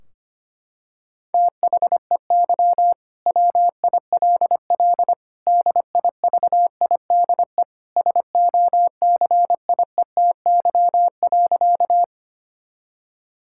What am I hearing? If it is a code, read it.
Morse code, heard as a keyed tone, about 25 wpm, 700 Hz. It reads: THEY WILL DIVIDE SOCIETY.
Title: Sentences from Top 1600 Words 25wpm